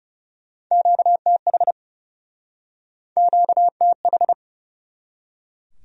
Morse code, heard as a keyed tone, about 40 wpm, 700 Hz. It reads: QTH QTH